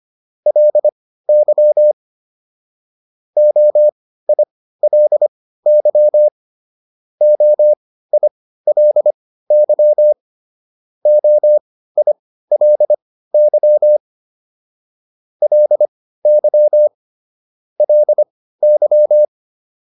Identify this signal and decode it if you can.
Morse code, heard as a keyed tone, about 25 wpm, spaced as for 15 wpm, 600 Hz. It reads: LY OILY OILY OILY LY LY